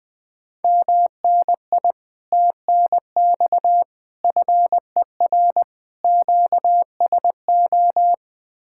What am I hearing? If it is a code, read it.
Morse code, heard as a keyed tone, about 20 wpm, 700 Hz. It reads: MNI TNX FER QSO
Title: Speed-Racing Common QSO Phrases Encore - 17wpm